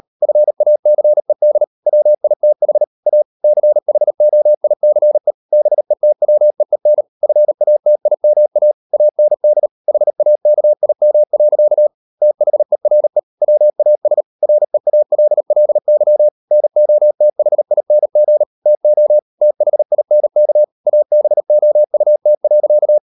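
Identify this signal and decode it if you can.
Morse code, heard as a keyed tone, about 38 wpm, 600 Hz. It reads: FACED WITH A CHOICE BETWEEN FATIMA AND HAKIM. THERE WAS REALLY NOTHING TO THINK ABOUT.
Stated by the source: Morse code practice at 38 WPM, sending Ken Follett's novel Never in English, from CW med Gnister.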